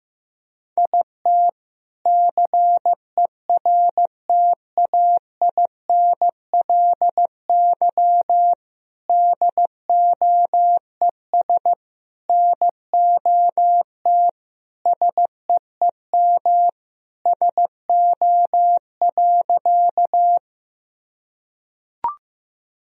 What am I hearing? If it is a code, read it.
Morse code, heard as a keyed tone, about 15 wpm, 700 Hz. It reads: IT CERTAINLY DOES NOT SEEM SO.
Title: Sentences from Top 1000 Words - Review Part 1 - 15wpm